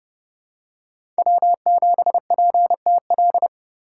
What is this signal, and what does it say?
Morse code, heard as a keyed tone, about 30 wpm, 700 Hz. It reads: W7PTL
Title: USA Call Signs - 5 Characters - 3x Character Spacing - 30wpm